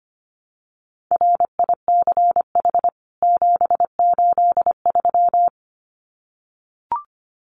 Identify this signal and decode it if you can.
Morse code, heard as a keyed tone, about 25 wpm, 700 Hz. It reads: RICH 783
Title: CWT Exchange 25wpm